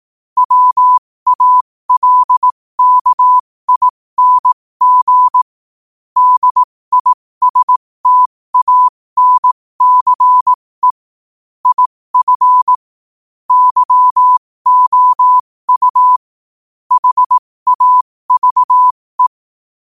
Morse code, heard as a keyed tone, about 18 wpm, spaced as for 15 wpm, 1000 Hz. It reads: WALKING DISTANCE IF YOU HAVE
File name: qotdfiles / 15wpm